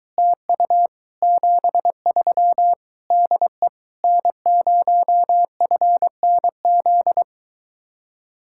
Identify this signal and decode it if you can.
Morse code, heard as a keyed tone, about 23 wpm, 700 Hz. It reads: TU 73 DE N0FNZ